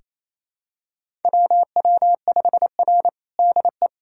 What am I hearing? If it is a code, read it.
Morse code, heard as a keyed tone, about 28 wpm, 700 Hz. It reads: WW5R DE
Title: USA Callsign DE Callsign - 4 Characters - 28wpm